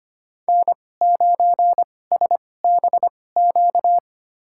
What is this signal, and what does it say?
Morse code, heard as a keyed tone, about 25 wpm, spaced as for 18 wpm, 700 Hz. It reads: N9SBQ
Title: USA Call Signs - 5 Characters - 2x Character Spacing - 25wpm